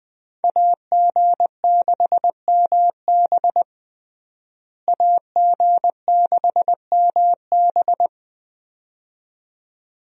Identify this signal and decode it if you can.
Morse code, heard as a keyed tone, about 20 wpm, 700 Hz. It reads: AG6MB AG6MB
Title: USA Call Signs - 5 Characters - Repeated 2x - 20wpm